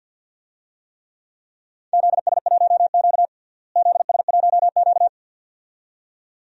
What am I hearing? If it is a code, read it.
Morse code, heard as a keyed tone, about 51 wpm, 700 Hz. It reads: ZS1Q ZS1Q
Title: DXCC-100 Call Signs - 4 Characters - Repeated 2x - 50wpm